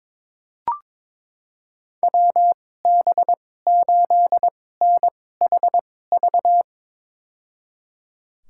Morse code, heard as a keyed tone, about 22 wpm, spaced as for 16 wpm, 700 Hz. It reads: WB8NHV
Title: USA Call Signs - 2x Character Spacing - 22wpm